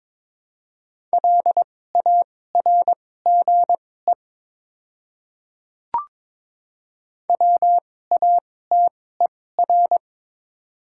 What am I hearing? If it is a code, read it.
Morse code, heard as a keyed tone, about 22 wpm, spaced as for 16 wpm, 700 Hz. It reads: LARGE WATER
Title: 2x Character Spacing - Top 200 Words - 22wpm